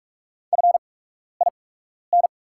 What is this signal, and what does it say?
Morse code, heard as a keyed tone, about 44 wpm, spaced as for 12 wpm, 700 Hz. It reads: FIN
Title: 8x Character Spacing - Top 200 Words - 45wpm